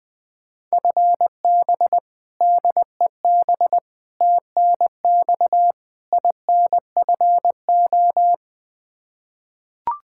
Morse code, heard as a keyed tone, about 20 wpm, 700 Hz. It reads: FB DEB TNX INFO
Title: Practice QSO - Exchange 3 - More Info - 20wpm